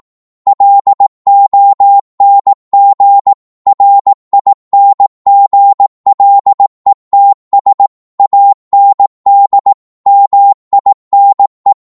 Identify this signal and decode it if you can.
Morse code, heard as a keyed tone, about 18 wpm, 800 Hz. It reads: LONG RINGLETS AND MINE